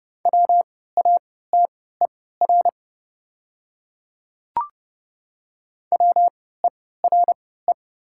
Morse code, heard as a keyed tone, about 30 wpm, spaced as for 17 wpm, 700 Hz. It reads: WATER WERE
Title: 3x Character Spacing - Top 100 Words - 30wpm